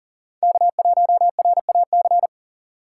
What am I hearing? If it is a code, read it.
Morse code, heard as a keyed tone, about 40 wpm, 700 Hz. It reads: K1RAC
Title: USA Call Signs - 5 Characters - 6x Character Spacing - 40wpm